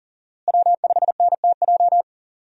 Morse code, heard as a keyed tone, about 38 wpm, 700 Hz. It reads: W5NTJ
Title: USA Call Signs - 5 Characters - Repeated 3x - 40wpm